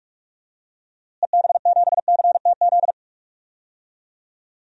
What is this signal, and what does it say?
Morse code, heard as a keyed tone, about 44 wpm, 700 Hz. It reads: EB7CTZ
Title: DXCC-100 Call Signs - 6 Characters - Repeated 3x - 45wpm